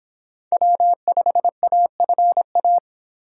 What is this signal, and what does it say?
Morse code, heard as a keyed tone, about 26 wpm, 700 Hz. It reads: W5AFA